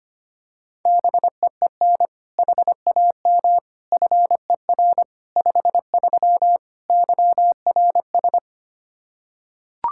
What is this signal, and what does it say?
Morse code, heard as a keyed tone, about 25 wpm, 700 Hz. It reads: BEEN HAM FER 53 YRS
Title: Practice QSO - Exchange 3 - More Info - 25wpm